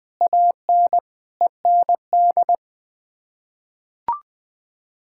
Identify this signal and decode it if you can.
Morse code, heard as a keyed tone, about 21 wpm, 700 Hz. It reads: AN END
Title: Sets of 8 Words - 20wpm